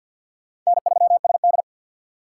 Morse code, heard as a keyed tone, about 50 wpm, 700 Hz. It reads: N3SD
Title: USA Call Signs - 3x Character Spacing - 50wpm